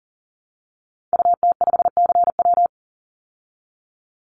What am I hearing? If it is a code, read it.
Morse code, heard as a keyed tone, about 40 wpm, 700 Hz. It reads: UT5CW
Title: Call Signs Repeated 2x - 40wpm